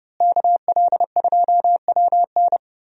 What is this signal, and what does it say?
Morse code, heard as a keyed tone, about 28 wpm, 700 Hz. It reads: KL2WN
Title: USA Call Signs - 5 Characters - Repeated 3x - 30wpm